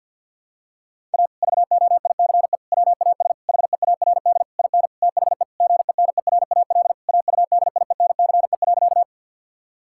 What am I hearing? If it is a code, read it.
Morse code, heard as a keyed tone, about 50 wpm, 700 Hz. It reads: A VOICE WAS HEARD IN THE GENERAL AUDIENCE.
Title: Sentences from Top 1000 Words - Review Part 5 - 50wpm